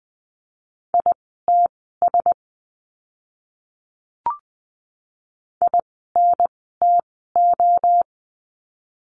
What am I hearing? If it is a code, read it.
Morse code, heard as a keyed tone, about 20 wpm, spaced as for 14 wpm, 700 Hz. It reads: ITS INTO
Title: 061 Lesson 11f - Letter h - Words 20wpm